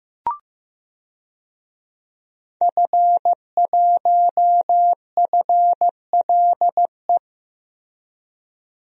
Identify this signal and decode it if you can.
Morse code, heard as a keyed tone, about 15 wpm, 700 Hz. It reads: F1FLE
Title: DXCC-100 Call Signs - 5 Characters - Repeated 2x - 15wpm